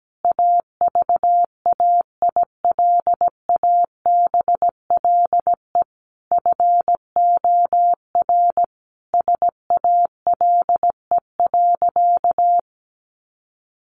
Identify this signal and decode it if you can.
Morse code, heard as a keyed tone, about 17 wpm, 700 Hz. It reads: AVAILABLE FOR SALE.
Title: Sentences from Top 1400 Words 17wpm